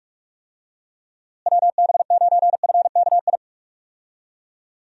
Morse code, heard as a keyed tone, about 44 wpm, 700 Hz. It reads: WB9FKI
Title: USA Call Signs - 6 Characters - Repeated 3x - 45wpm